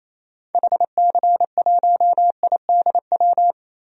Morse code, heard as a keyed tone, about 28 wpm, 700 Hz. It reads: HC1IDW
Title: DXCC-100 Call Signs - 6 Characters - Repeated 3x - 28wpm